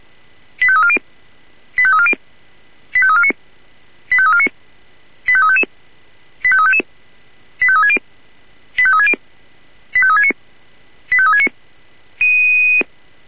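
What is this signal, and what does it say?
Five-tone selective call (ZVEI), heard as quick runs of five tones, tones of about 70 ms. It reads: 85340 85340 85349 85349 85355 85355 85360 85360 85369 85369
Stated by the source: ZVEI five-tone call 85355 received on an SDR.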